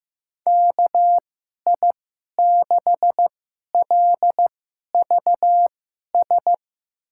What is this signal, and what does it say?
Morse code, heard as a keyed tone, about 15 wpm, spaced as for 11 wpm, 700 Hz. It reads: KI6LVS
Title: USA Call Signs - 2x Character Spacing - 15wpm